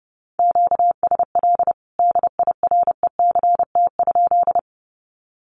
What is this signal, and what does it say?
Morse code, heard as a keyed tone, about 30 wpm, 700 Hz. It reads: QSL DIRECT?
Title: Common QSO Phrases 30wpm